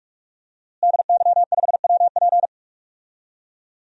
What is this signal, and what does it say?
Morse code, heard as a keyed tone, about 44 wpm, 700 Hz. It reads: DY5WP
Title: DXCC-100 Call Signs - 5 Characters - Repeated 3x - 45wpm